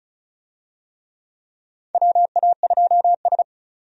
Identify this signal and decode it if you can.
Morse code, heard as a keyed tone, about 36 wpm, 700 Hz. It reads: WA2S
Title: USA Call Signs - 4 Characters - 35wpm